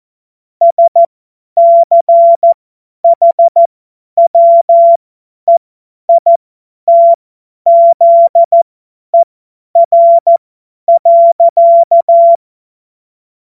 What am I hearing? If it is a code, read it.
Morse code, heard as a keyed tone, about 14 wpm, spaced as for 10 wpm, 680 Hz. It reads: SCHWEITZER.
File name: Morsekurs / Witze / Witz 14. Dezember 2021, 50 F70